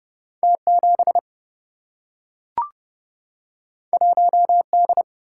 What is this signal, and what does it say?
Morse code, heard as a keyed tone, about 30 wpm, 700 Hz. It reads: T7 1D
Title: Single-Letter Single-Number 30wpm